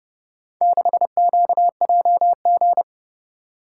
Morse code, heard as a keyed tone, about 29 wpm, 700 Hz. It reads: 6QJG